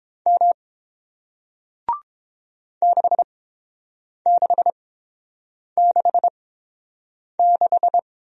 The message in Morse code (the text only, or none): M 6666